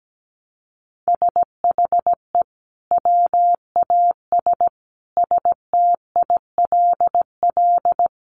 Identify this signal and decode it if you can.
Morse code, heard as a keyed tone, about 17 wpm, 700 Hz. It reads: SHE WAS STILL
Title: Sentences from Top 500 Words 17wpm